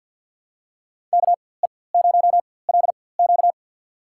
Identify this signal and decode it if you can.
Morse code, heard as a keyed tone, about 46 wpm, spaced as for 23 wpm, 700 Hz. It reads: KE0LQ
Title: USA Call Signs - 4x Character Spacing - 50wpm